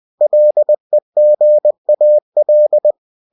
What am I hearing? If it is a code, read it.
Morse code, heard as a keyed tone, about 20 wpm, 600 Hz. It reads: LEGAL